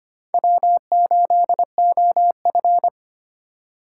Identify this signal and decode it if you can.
Morse code, heard as a keyed tone, about 25 wpm, 700 Hz. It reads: W8OF
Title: USA Call Signs - 4 Characters - 2x Character Spacing - 25wpm